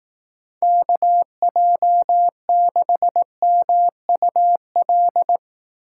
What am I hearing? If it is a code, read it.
Morse code, heard as a keyed tone, about 18 wpm, 700 Hz. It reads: KJ6MUL
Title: Speed Racing - USA 2x3 Call Signs 15wpm